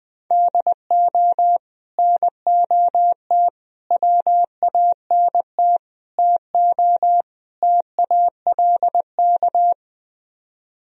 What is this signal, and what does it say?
Morse code, heard as a keyed tone, about 20 wpm, 700 Hz. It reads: DO NOT WANT TO TALK
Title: Speed-Racing - Sets of 5 Words 20wpm